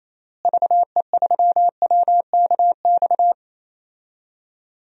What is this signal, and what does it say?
Morse code, heard as a keyed tone, about 28 wpm, 700 Hz. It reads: VE3WKX
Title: DXCC-100 Call Signs - 6 Characters - Repeated 3x - 28wpm